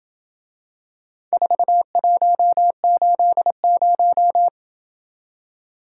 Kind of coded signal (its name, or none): Morse code